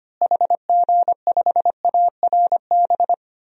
Morse code, heard as a keyed tone, about 25 wpm, 700 Hz. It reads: HG5ARB
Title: DXCC-100 Call Signs - 6 Characters - Repeated 2x - 25wpm